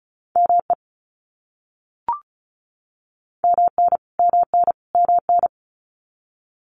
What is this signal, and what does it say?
Morse code, heard as a keyed tone, about 34 wpm, 700 Hz. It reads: ME MN MN MN
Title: US States Repeated 3x 35wpm